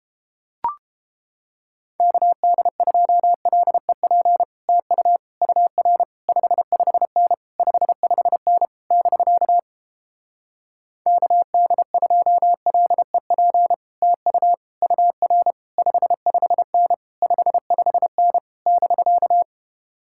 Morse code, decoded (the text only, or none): KD2LEP TU UR 55N 55N <BK> KD2LEP TU UR 55N 55N <BK>